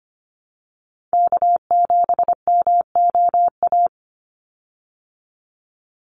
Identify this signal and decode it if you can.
Morse code, heard as a keyed tone, about 25 wpm, 700 Hz. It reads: K7MOA
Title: Call Signs Repeated 2x - 25wpm